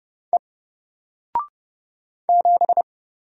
Morse code, heard as a keyed tone, about 29 wpm, 700 Hz. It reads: E 7
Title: Single Letter-Number Alt 30wpm